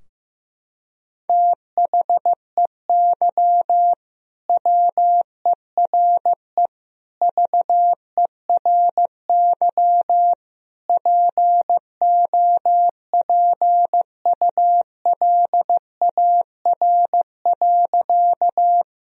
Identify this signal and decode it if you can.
Morse code, heard as a keyed tone, about 16 wpm, 700 Hz. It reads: THEY WERE VERY POPULAR.
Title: Sentences from Top 1000 Words - Review Part 3 - 15wpm